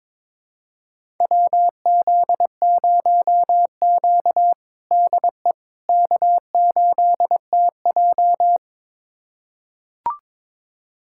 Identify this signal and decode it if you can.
Morse code, heard as a keyed tone, about 22 wpm, 700 Hz. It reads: WZ0Q DE K8TJ
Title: USA Callsign DE Callsign - 4 Characters - 22wpm